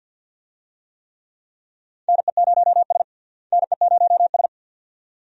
Morse code, heard as a keyed tone, about 48 wpm, 700 Hz. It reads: NE0S NE0S
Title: USA Call Signs - 4 Characters - Repeated 2x - 50wpm